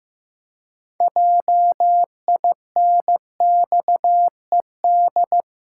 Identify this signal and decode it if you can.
Morse code, heard as a keyed tone, about 16 wpm, 700 Hz. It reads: JINXED
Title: Pangram Challenge v2 - 15wpm